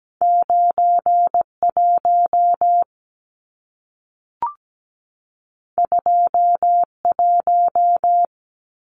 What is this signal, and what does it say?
Morse code, heard as a keyed tone, about 17 wpm, 700 Hz. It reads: 91 21
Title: Numbers - 2-Digit 17wpm